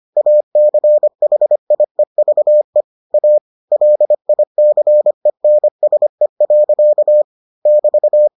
Morse code, heard as a keyed tone, about 25 wpm, 600 Hz. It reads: ACHIEVE A LICENSE. <BT>